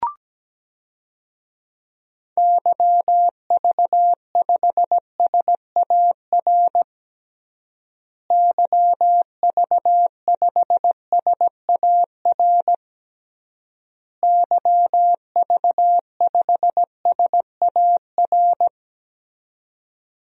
Morse code, heard as a keyed tone, about 17 wpm, 700 Hz. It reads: YV5SAR YV5SAR YV5SAR
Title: DXCC-100 Call Signs - 6 Characters - Repeated 3x - 17wpm